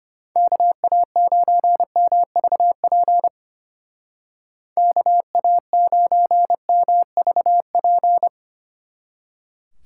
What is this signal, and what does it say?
Morse code, heard as a keyed tone, about 30 wpm, 700 Hz. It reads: KA9MVP KA9MVP
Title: Speed Racing - USA 2x3 Call Signs 25wpm